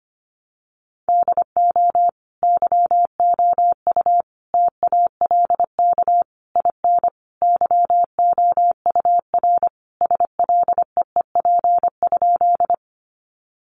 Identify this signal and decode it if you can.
Morse code, heard as a keyed tone, about 25 wpm, 700 Hz. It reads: DO YOU TALK IN YOUR SLEEP?
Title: Sentences from Top 500 Words 25wpm